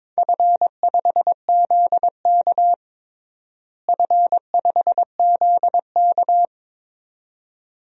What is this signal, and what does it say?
Morse code, heard as a keyed tone, about 21 wpm, 700 Hz. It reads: F5ZK F5ZK
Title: DXCC-100 Call Signs - 4 Characters - Repeated 2x - 22wpm